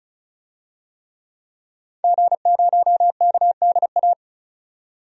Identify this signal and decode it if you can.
Morse code, heard as a keyed tone, about 34 wpm, 700 Hz. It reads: G0KDA
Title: DXCC-100 Call Signs - 5 Characters - 35wpm